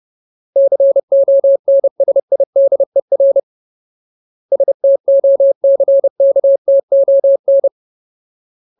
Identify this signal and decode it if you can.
Morse code, heard as a keyed tone, about 30 wpm, 550 Hz. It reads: CONSIDER STOCKTON
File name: Ultimate Word List / 30WPM Ultimate